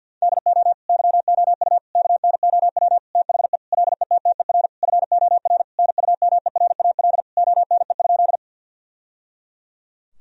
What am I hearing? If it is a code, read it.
Morse code, heard as a keyed tone, about 50 wpm, 700 Hz. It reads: DO YOU KNOW THE LETTER FOR NUMERAL ONE?
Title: Sentences from Top 1000 Words - Review Part 2 - 50wpm